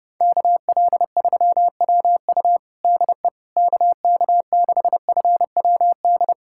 Morse code, heard as a keyed tone, about 30 wpm, 690 Hz. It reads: KL3WU DE KK6FWD
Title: USA Callsign DE Callsign 30wpm